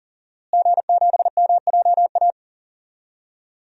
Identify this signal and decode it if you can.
Morse code, heard as a keyed tone, about 40 wpm, 700 Hz. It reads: G7MJA
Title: DXCC-100 Call Signs - 5 Characters - Repeated 2x - 40wpm